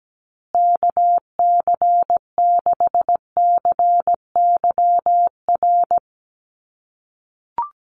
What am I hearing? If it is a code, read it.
Morse code, heard as a keyed tone, about 17 wpm, 700 Hz. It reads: KC6CYR
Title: USA Call Signs Repeated 2x 17wpm